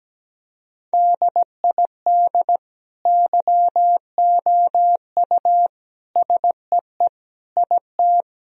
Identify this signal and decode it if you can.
Morse code, heard as a keyed tone, about 17 wpm, 700 Hz. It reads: DID YOU SEE IT
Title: Sentences from Top 1000 Words - Review Part 1 - 17wpm